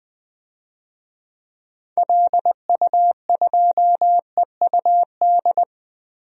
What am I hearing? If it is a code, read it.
Morse code, heard as a keyed tone, about 20 wpm, 700 Hz. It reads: LU2EUD
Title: DXCC-100 Call Signs - 6 Characters - 20wpm